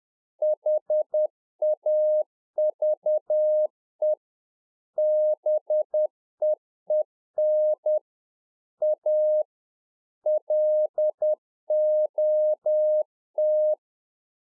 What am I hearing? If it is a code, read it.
Morse code, heard as a keyed tone, about 10 wpm, 610 Hz. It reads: HAVE BEEN A LOT